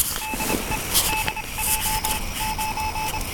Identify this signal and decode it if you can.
Morse code, heard as a keyed tone, about 25 wpm, spaced as for 22 wpm, 870 Hz, with noise. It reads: DE DJ9